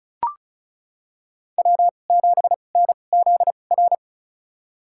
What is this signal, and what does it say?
Morse code, heard as a keyed tone, about 34 wpm, spaced as for 25 wpm, 700 Hz. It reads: W7NZR
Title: USA Call Signs - 5 Characters - 2x Character Spacing - 35wpm